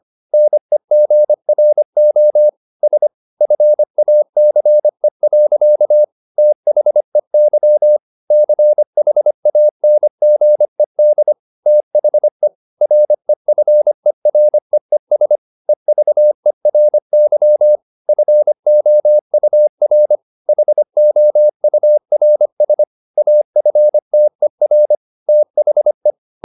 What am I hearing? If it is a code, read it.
Morse code, heard as a keyed tone, about 25 wpm, 600 Hz. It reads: NEGRO S FACE. THEY CHANGED THE REFEREES EVERY FOUR HOURS AFTER THE